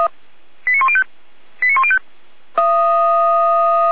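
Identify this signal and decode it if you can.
Five-tone selective call (ZVEI), heard as quick runs of five tones, tones of about 70 ms. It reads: 89105 89105
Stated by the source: Received on SDR; ZVEI five-tone call.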